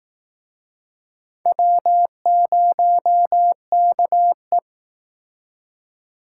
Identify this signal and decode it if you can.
Morse code, heard as a keyed tone, about 18 wpm, 700 Hz. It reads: W0KE
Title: Speed Racing - USA 1x2 Call Signs 15wpm